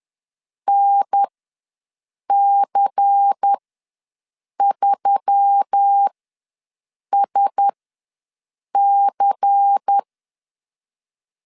Morse code, heard as a keyed tone, about 10 wpm, spaced as for 6 wpm, 800 Hz. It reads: NC3SC
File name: LEZIONI / 5 LBVC / 5 LBVC ws 10wpm 3733